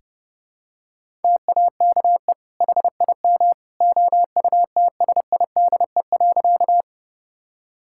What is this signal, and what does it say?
Morse code, heard as a keyed tone, about 32 wpm, 700 Hz. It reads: TAKE HIM OUTSIDE.
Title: Sentences from Top 1000 Words - Review Part 6 - 30wpm